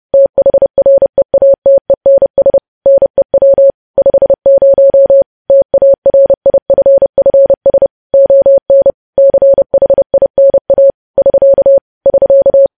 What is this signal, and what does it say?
Morse code, heard as a keyed tone, about 30 wpm, 550 Hz. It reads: THREATENS NEW 50 TARIFFS ON CHINA <SK> <SK>
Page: news headlines in morse code at 30 wpm